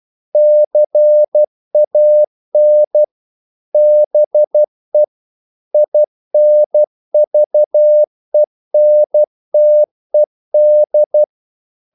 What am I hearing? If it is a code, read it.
Morse code, heard as a keyed tone, about 12 wpm, 600 Hz. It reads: CAN BE INVENTED
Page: English quotations in CW (telegraphy - morse)